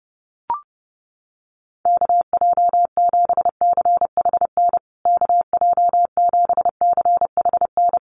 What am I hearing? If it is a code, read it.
Morse code, heard as a keyed tone, about 30 wpm, 700 Hz. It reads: KJ7CHN KJ7CHN
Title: USA Call Signs Repeated 3x 30wpm